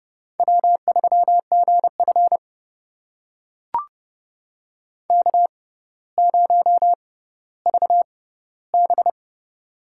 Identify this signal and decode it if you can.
Morse code, heard as a keyed tone, about 31 wpm, spaced as for 15 wpm, 700 Hz. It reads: W3GF K0VB
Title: USA Call Signs - 4 Characters - 6x Character Spacing - 30wpm